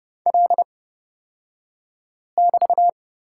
Morse code, heard as a keyed tone, about 31 wpm, spaced as for 5 wpm, 700 Hz. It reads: L<BT>
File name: Lesson 36d - BT - Two Characters 30wpm